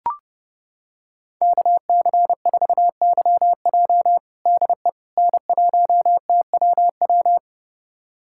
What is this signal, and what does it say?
Morse code, heard as a keyed tone, about 30 wpm, 700 Hz. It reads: KC4YJ DE N1TWW
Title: USA Callsign DE Callsign - 5 Characters - 30wpm